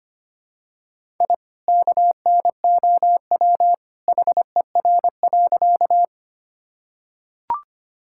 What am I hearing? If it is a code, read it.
Morse code, heard as a keyed tone, about 25 wpm, 700 Hz. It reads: I KNOW HER.
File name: practice / 25wpm / Sentences with 2-Word Phrases - 25wpm